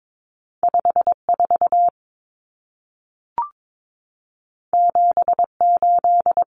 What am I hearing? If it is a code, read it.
Morse code, heard as a keyed tone, about 22 wpm, 700 Hz. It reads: 54 78
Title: Numbers - 2-Digit 22wpm